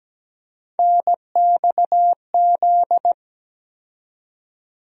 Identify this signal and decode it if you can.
Morse code, heard as a keyed tone, about 17 wpm, 700 Hz. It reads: NXZ
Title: USA Call Sign Postfixes - 3 Letters 17wpm